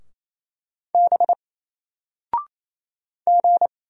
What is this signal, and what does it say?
Morse code, heard as a keyed tone, about 27 wpm, 700 Hz. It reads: B G
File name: Single-Letter-Number-28wpm